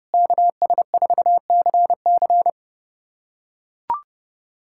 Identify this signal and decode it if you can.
Morse code, heard as a keyed tone, about 30 wpm, 700 Hz. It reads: KS4CC
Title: USA Callsign DE Callsign 30wpm